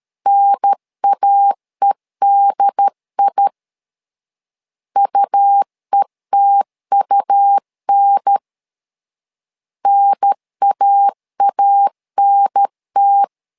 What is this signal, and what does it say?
Morse code, heard as a keyed tone, about 13 wpm, 800 Hz. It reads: NAEDI UETUN NAANT
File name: LEZIONI / 2 UHDN / 2 UHDN 12wpm 1456